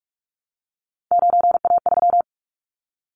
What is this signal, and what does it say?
Morse code, heard as a keyed tone, about 44 wpm, 700 Hz. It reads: 9A3
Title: Callsign Prefixes for Top 25 DXCC - Repeated 3x - 45wpm